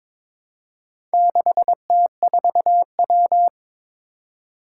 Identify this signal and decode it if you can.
Morse code, heard as a keyed tone, about 22 wpm, 700 Hz. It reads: 6T4W
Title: Four Challenging Letter-Number Combinations 22wpm